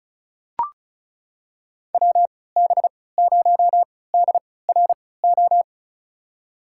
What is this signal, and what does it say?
Morse code, heard as a keyed tone, about 36 wpm, spaced as for 20 wpm, 700 Hz. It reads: WB0DRO
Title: USA Call Signs - 6 Characters - 3x Character Spacing - 35wpm